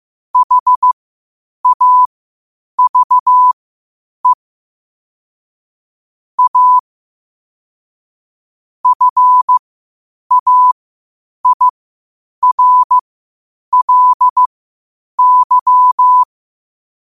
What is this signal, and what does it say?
Morse code, heard as a keyed tone, about 15 wpm, spaced as for 8 wpm, 1000 Hz. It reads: HAVE A FAIRLY
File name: qotdfiles / 8wpm